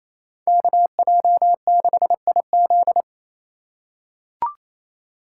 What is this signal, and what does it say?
Morse code, heard as a keyed tone, about 28 wpm, 700 Hz. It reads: KJ6IZ
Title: USA Call Signs - 5 Characters - 4x Character Spacing - 28wpm